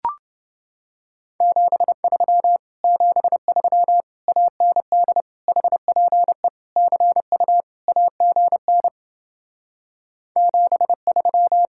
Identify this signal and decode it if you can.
Morse code, heard as a keyed tone, about 30 wpm, 700 Hz. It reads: 73 73 AND HPE CU AGN 73